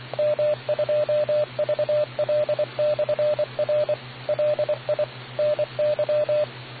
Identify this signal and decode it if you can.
Morse code, heard as a keyed tone, about 24 wpm, 600 Hz, with radio noise. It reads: M2VL/R LI NY